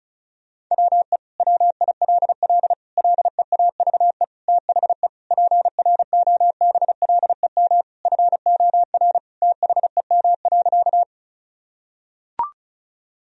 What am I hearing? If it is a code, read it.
Morse code, heard as a keyed tone, about 34 wpm, 700 Hz. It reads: WE WILL LEAVE THE PROBLEM FOR THEM.
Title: Sentences from Top 1000 Words - Review Part 2 - 35wpm